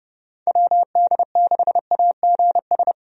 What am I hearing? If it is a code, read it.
Morse code, heard as a keyed tone, about 30 wpm, 700 Hz. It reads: WD6AGS
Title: USA Call Signs - 6 Characters - Repeated 3x - 30wpm